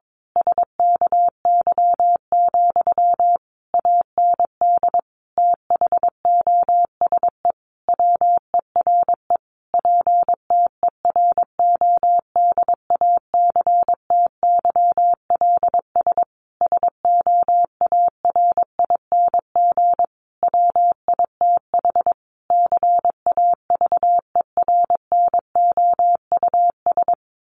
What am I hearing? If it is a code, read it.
Morse code, heard as a keyed tone, about 22 wpm, 700 Hz. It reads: SKY, AND THOSE WERE PTERODACTYLS SOARING WITH CAVERNOUS
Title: A Sound Of Thunder 22wpm